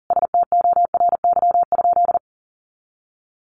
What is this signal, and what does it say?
Morse code, heard as a keyed tone, about 40 wpm, 700 Hz. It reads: STORY?
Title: Sentences from Top 1200 Words 40wpm